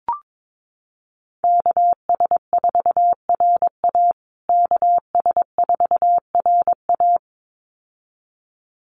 Morse code, heard as a keyed tone, about 22 wpm, 700 Hz. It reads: KS4RA KS4RA
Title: USA Call Signs Repeated 2x 22wpm